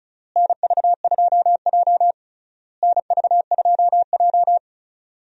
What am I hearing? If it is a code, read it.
Morse code, heard as a keyed tone, about 36 wpm, 700 Hz. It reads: NV2J NV2J
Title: USA Call Signs - 4 Characters - Repeated 3x - 35wpm